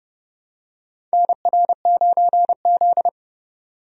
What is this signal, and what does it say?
Morse code, heard as a keyed tone, about 30 wpm, 700 Hz. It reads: NR9Z
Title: USA Call Signs - 4 Characters - 30wpm